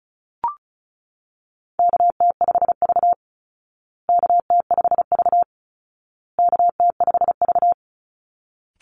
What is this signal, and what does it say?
Morse code, heard as a keyed tone, about 34 wpm, 700 Hz. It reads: KT5V KT5V KT5V
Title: Call Signs Repeated 2x - 35wpm